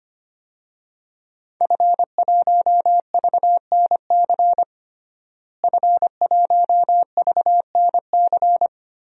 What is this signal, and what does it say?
Morse code, heard as a keyed tone, about 25 wpm, 700 Hz. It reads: F1VNC F1VNC
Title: DXCC-100 Call Signs - 5 Characters - Repeated 3x - 25wpm